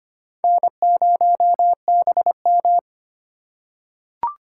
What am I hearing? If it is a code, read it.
Morse code, heard as a keyed tone, about 25 wpm, 700 Hz. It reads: N0BM
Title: USA Call Signs - 4 Characters - 4x Character Spacing - 25wpm